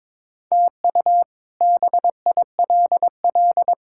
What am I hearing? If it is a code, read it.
Morse code, heard as a keyed tone, about 22 wpm, 700 Hz. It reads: TU BILL
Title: K1USN SST QSO Practice - 22wpm